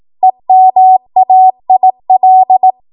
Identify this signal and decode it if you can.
Morse code, heard as a keyed tone, about 18 wpm, 750 Hz. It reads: EMAIL
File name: Archive / 18 WPM / 18WPM